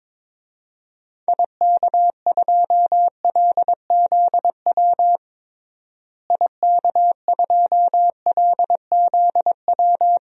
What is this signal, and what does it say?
Morse code, heard as a keyed tone, about 22 wpm, 700 Hz. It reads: IK2LZW IK2LZW
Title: DXCC-100 Call Signs - 6 Characters - Repeated 2x - 22wpm